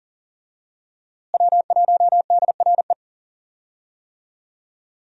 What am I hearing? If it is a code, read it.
Morse code, heard as a keyed tone, about 40 wpm, 700 Hz. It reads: W1DRE
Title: USA Call Signs - 5 Characters - Repeated 2x - 40wpm